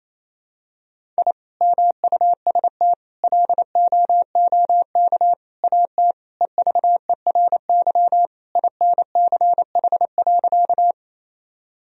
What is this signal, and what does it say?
Morse code, heard as a keyed tone, about 28 wpm, 700 Hz. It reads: I MUST LOOK AT EVERY INCH.
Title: Sentences from Top 1000 Words - Review Part 3 - 28wpm